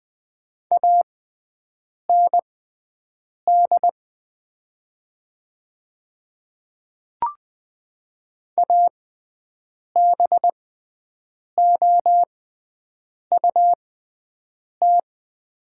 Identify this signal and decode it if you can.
Morse code, heard as a keyed tone, about 20 wpm, spaced as for 7 wpm, 700 Hz. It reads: AND ABOUT
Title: 6x Character Spacing - Top 100 Words - 20wpm